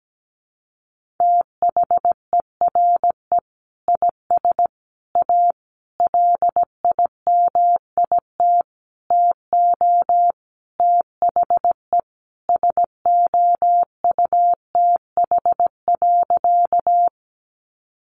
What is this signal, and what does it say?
Morse code, heard as a keyed tone, about 17 wpm, 700 Hz. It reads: THERE IS A LIMIT TO THE SOUTH.
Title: Sentences from Top 1500 Words 17wpm